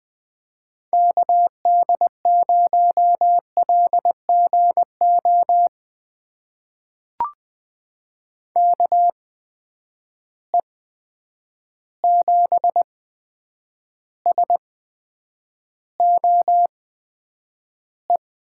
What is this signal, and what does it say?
Morse code, heard as a keyed tone, about 20 wpm, spaced as for 9 wpm, 700 Hz. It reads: KD0LGO KE7SOE
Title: USA Call Signs - 6 Characters - 8x Character Spacing - 20wpm